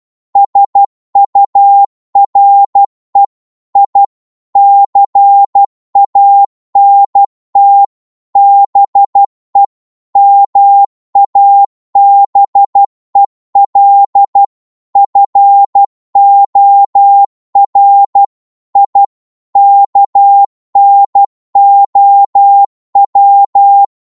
Morse code, heard as a keyed tone, about 12 wpm, 800 Hz. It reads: SURE I CANT BE MABEL FOR I KNOW
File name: 12wpm